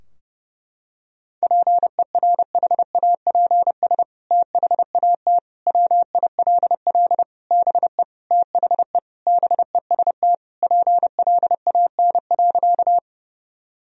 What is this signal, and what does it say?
Morse code, heard as a keyed tone, about 32 wpm, 700 Hz. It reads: PERHAPS THAT WILL BE THE BEST PLAN.
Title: Sentences from Top 1000 Words - Review Part 2 - 30wpm